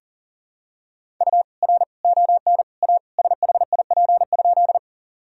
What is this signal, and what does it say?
Morse code, heard as a keyed tone, about 40 wpm, 700 Hz. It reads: U R ON A SHIP?
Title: Sentences from Top 400 Words and Abbreviations 40wpm